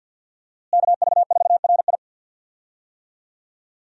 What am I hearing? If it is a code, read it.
Morse code, heard as a keyed tone, about 51 wpm, 700 Hz. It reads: KV4RI
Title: USA Call Signs - 5 Characters - Repeated 3x - 50wpm